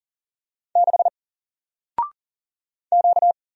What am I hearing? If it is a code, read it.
Morse code, heard as a keyed tone, about 40 wpm, 700 Hz. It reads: 6 Q
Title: Single Letter-Number Alt 40wpm